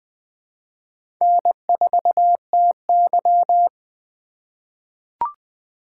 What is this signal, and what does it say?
Morse code, heard as a keyed tone, about 21 wpm, 700 Hz. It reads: N4TY
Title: USA Call Signs - 4 Characters - 4x Character Spacing - 20wpm